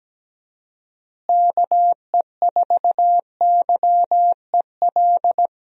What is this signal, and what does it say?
Morse code, heard as a keyed tone, about 17 wpm, 700 Hz. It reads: KE4YEL
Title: USA Call Signs - 6 Characters - Repeated 3x - 17wpm